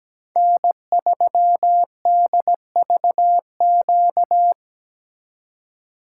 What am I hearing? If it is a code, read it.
Morse code, heard as a keyed tone, about 17 wpm, 700 Hz. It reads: N3DVQ
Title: USA Call Signs - 5 Characters - 4x Character Spacing - 17wpm